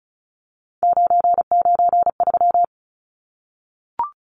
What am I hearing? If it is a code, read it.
Morse code, heard as a keyed tone, about 34 wpm, 700 Hz. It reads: 993
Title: Numbers - 3-Digit 35wpm